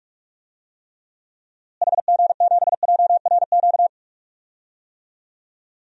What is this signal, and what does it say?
Morse code, heard as a keyed tone, about 41 wpm, 700 Hz. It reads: HG7JRQ
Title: DXCC-100 Call Signs - 6 Characters - Repeated 2x - 45wpm